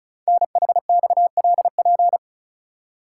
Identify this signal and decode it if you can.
Morse code, heard as a keyed tone, about 36 wpm, 700 Hz. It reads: NHXLP
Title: Code Groups - 5 Letter:Numbers - 35wpm